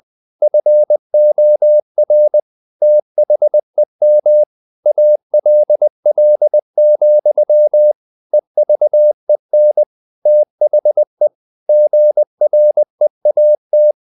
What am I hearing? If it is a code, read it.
Morse code, heard as a keyed tone, about 20 wpm, 600 Hz. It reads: FOR THEM ALL, EVEN THE GREAT
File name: the old man and the sea / the old man and the sea part 1 20wpm